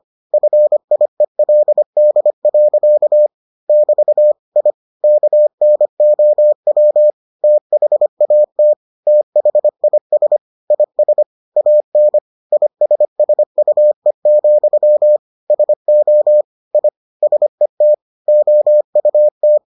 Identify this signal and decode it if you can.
Morse code, heard as a keyed tone, about 25 wpm, 600 Hz. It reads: FIELD. <BT> I KNOW THAT THIS IS AN ISSUE, SO I SET OUT